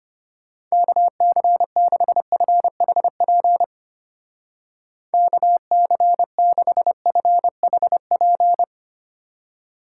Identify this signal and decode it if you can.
Morse code, heard as a keyed tone, about 31 wpm, 700 Hz. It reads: KC6FHP KC6FHP